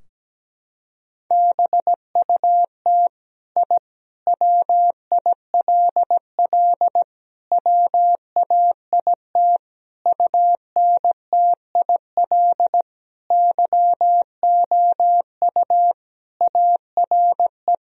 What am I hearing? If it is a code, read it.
Morse code, heard as a keyed tone, about 17 wpm, 700 Hz. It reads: BUT I WILL WAIT UNTIL YOU ARE
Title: Sentences from Top 1000 Words - Review Part 3 - 17wpm